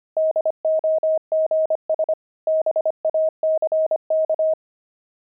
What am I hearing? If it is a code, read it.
Morse code, heard as a keyed tone, about 25 wpm, 630 Hz. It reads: DOGS BACK
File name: pangram-fs-25-cs-25-fox